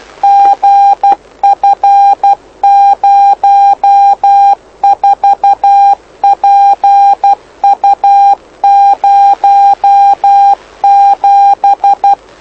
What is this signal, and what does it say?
Morse code, heard as a keyed tone, about 12 wpm, 800 Hz. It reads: GF04PU07
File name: lu4aao / lu7aa 09dic2011 exp planeador vuelo1